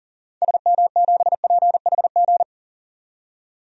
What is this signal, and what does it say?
Morse code, heard as a keyed tone, about 39 wpm, 700 Hz. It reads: SM7PHG
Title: DXCC-100 Call Signs - 6 Characters - 40wpm